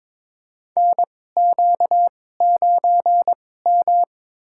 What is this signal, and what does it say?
Morse code, heard as a keyed tone, about 22 wpm, spaced as for 16 wpm, 700 Hz. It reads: NQ9M
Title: USA Call Signs - 2x Character Spacing - 22wpm